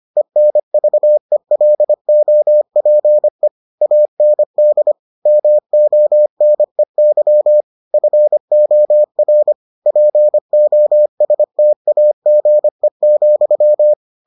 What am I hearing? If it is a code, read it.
Morse code, heard as a keyed tone, about 25 wpm, 600 Hz. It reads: ENVELOPE AND MONEY FOR POSTAGE,